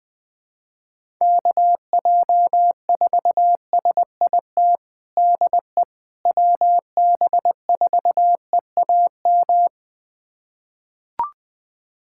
Morse code, heard as a keyed tone, about 20 wpm, 700 Hz. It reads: KJ4SIT DE WB4EAM
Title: USA Callsign DE Callsign - 6 Characters - 20wpm